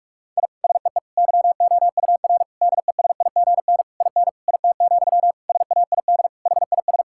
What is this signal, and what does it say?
Morse code, heard as a keyed tone, about 46 wpm, 700 Hz. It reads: I SEE YOUR DESIGN IN IT, SAID HIS
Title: Sentences from Top 1000 Words - Review Part 5 - 45wpm